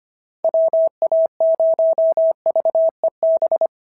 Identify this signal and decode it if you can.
Morse code, heard as a keyed tone, about 25 wpm, 690 Hz. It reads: WA0VEB
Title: SOTA QSO Practice 25wpm